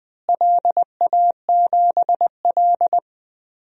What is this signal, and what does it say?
Morse code, heard as a keyed tone, about 20 wpm, 700 Hz. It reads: LA7L
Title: DXCC-100 Call Signs - 4 Characters - Repeated 2x - 20wpm